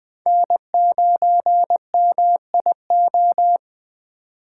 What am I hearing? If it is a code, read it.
Morse code, heard as a keyed tone, about 20 wpm, 700 Hz. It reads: N9MIO